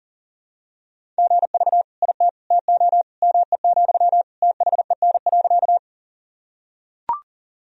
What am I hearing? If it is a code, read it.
Morse code, heard as a keyed tone, about 40 wpm, 700 Hz. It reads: GV IT TO ME, THEN.
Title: Sentences from Top 200 Words and Abbreviations 40wpm